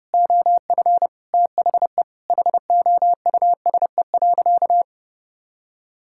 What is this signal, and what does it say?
Morse code, heard as a keyed tone, about 32 wpm, 700 Hz. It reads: OF THE HOUSE.
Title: Sentences from Top 1600 Words 30wpm